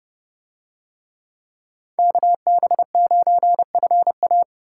Morse code, heard as a keyed tone, about 29 wpm, 700 Hz. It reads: KB9FA